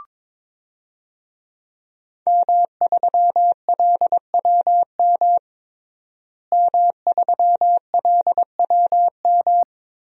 Morse code, heard as a keyed tone, about 22 wpm, 700 Hz. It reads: M3LWM M3LWM
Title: DXCC-100 Call Signs - 5 Characters - Repeated 2x - 22wpm